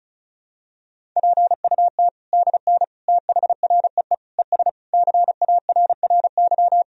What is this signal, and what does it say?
Morse code, heard as a keyed tone, about 36 wpm, 700 Hz. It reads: PUT DN THREE ES CARRY
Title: Sentences from Top 400 Words and Abbreviations 35wpm